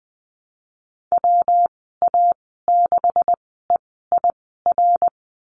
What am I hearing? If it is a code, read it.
Morse code, heard as a keyed tone, about 19 wpm, spaced as for 14 wpm, 700 Hz. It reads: WA6EIR
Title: 173 Lesson 28g - Number 9 - Call Signs 20wpm